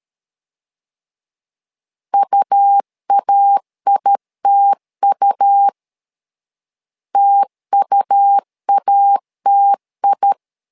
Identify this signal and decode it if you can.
Morse code, heard as a keyed tone, about 13 wpm, 800 Hz. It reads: UAITU TUATI
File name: LEZIONI / 2 UHDN / 2 U 12wpm 1338